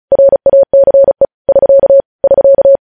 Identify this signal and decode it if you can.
Morse code, heard as a keyed tone, about 36 wpm, 550 Hz. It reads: RACE <SK> <SK>